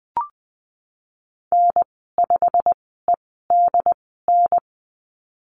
Morse code, heard as a keyed tone, about 20 wpm, spaced as for 15 wpm, 700 Hz. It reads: N5EDN
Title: 088 Lesson 15g - Number 5 - Call Signs 20wpm